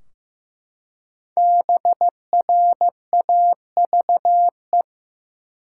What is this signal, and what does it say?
Morse code, heard as a keyed tone, about 15 wpm, 700 Hz. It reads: BRAVE
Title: Five Letter Words - 15wpm